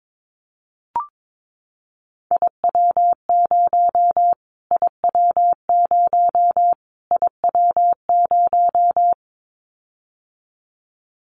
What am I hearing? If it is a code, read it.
Morse code, heard as a keyed tone, about 22 wpm, 700 Hz. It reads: IW0 IW0 IW0
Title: Callsign Prefixes for Top 25 DXCC - Repeated 3x - 22wpm